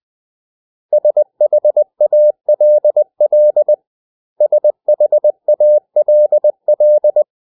Morse code, heard as a keyed tone, about 20 wpm, 600 Hz. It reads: SHALL SHALL
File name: cw / 100 Common English Words